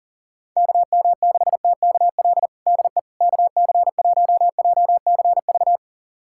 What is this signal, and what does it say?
Morse code, heard as a keyed tone, about 40 wpm, 700 Hz. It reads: KM6TKL DE KC1JCV
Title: USA Callsign DE Callsign 40wpm